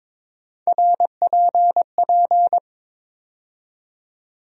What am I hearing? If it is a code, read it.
Morse code, heard as a keyed tone, about 22 wpm, 700 Hz. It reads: RPP
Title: USA Call Sign Postfixes - 3 Letters 22wpm